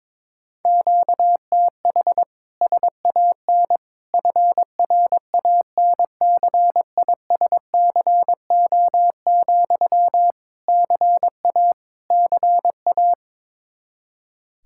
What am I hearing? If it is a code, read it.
Morse code, heard as a keyed tone, about 21 wpm, 700 Hz. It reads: QTH SAN FRANCISCO, CA CA
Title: K1USN SST QSO Practice 22wpm